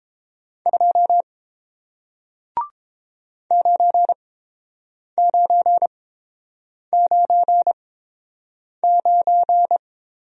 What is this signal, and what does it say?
Morse code, heard as a keyed tone, about 30 wpm, spaced as for 7 wpm, 690 Hz. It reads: 2 9999